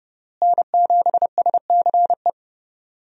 Morse code, heard as a keyed tone, about 30 wpm, 700 Hz. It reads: N7SCE
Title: USA Call Signs - 5 Characters - Repeated 3x - 30wpm